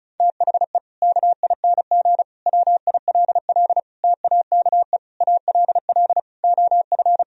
Morse code, heard as a keyed tone, about 34 wpm, 700 Hz. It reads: THE KING WILL TAKE ALL OF
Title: Sentences from Top 1000 Words - Review Part 4 - 35wpm